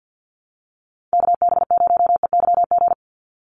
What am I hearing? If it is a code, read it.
Morse code, heard as a keyed tone, about 50 wpm, 700 Hz. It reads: KB0EYG
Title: USA Call Signs 50wpm